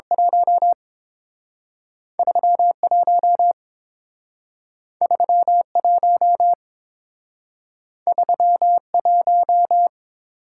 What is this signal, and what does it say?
Morse code, heard as a keyed tone, about 28 wpm, spaced as for 6 wpm, 700 Hz. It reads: 13 13 13 1